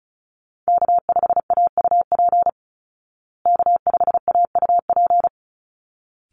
Morse code, heard as a keyed tone, about 34 wpm, 700 Hz. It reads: K5AUP K5AUP
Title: Call Signs Repeated 3x - 35wpm